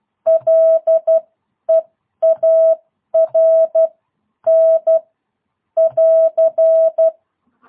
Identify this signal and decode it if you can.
Morse code, heard as a keyed tone, about 12 wpm, spaced as for 10 wpm, 650 Hz. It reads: LEARN <AR>
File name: jh8cbh / 2022cw